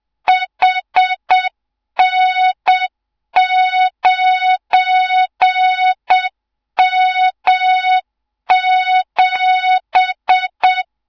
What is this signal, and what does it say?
Morse code, heard as a keyed tone, about 7 wpm, 3700 Hz, with noise. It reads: HN9M7
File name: Morze-HA2MN / hang / 24 GYAKORLAT